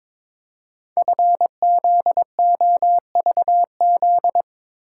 Morse code, heard as a keyed tone, about 23 wpm, 700 Hz. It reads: FZOVZ
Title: Code Groups - 5 Letters - 22wpm